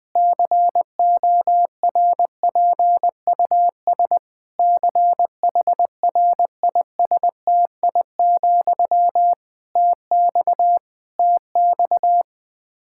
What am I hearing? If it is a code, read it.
Morse code, heard as a keyed tone, about 20 wpm, 700 Hz. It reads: CORPUS CHRISTI, TX TX
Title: K1USN SST QSO Practice 20wpm